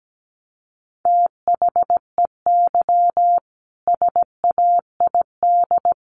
Morse code, heard as a keyed tone, about 17 wpm, 700 Hz. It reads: THEY SAID
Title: Sentences from Top 100 Words 17wpm